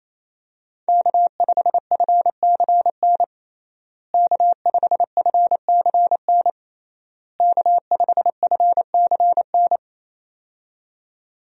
Morse code, heard as a keyed tone, about 28 wpm, 700 Hz. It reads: K5FCN K5FCN K5FCN
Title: USA Call Signs - 5 Characters - Repeated 3x - 28wpm